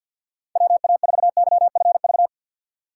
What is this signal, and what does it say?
Morse code, heard as a keyed tone, about 50 wpm, 700 Hz. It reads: WA4YFV
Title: USA Call Signs - 6 Characters - 4x Character Spacing - 50wpm